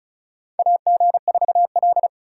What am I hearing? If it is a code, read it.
Morse code, heard as a keyed tone, about 36 wpm, 700 Hz. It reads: AG4L
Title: USA Call Signs - 4 Characters - 2x Character Spacing - 35wpm